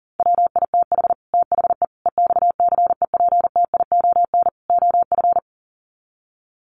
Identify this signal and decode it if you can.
Morse code, heard as a keyed tone, about 40 wpm, 700 Hz. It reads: WITH THE EXCEPTION OF
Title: Sets of 4 Words 40wpm